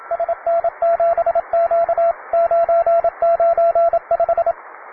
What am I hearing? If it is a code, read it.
Morse code, heard as a keyed tone, about 27 wpm, 680 Hz, with radio noise. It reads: SN7Q 995
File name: wae / cw beginner / py2rh cw